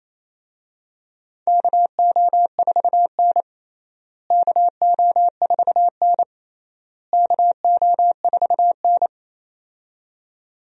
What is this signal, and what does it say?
Morse code, heard as a keyed tone, about 28 wpm, 700 Hz. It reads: KO4N KO4N KO4N
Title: USA Call Signs - 4 Characters - Repeated 3x - 28wpm